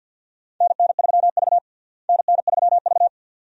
Morse code, heard as a keyed tone, about 46 wpm, 700 Hz. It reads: NN3V NN3V
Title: USA Call Signs - 4 Characters - Repeated 2x - 50wpm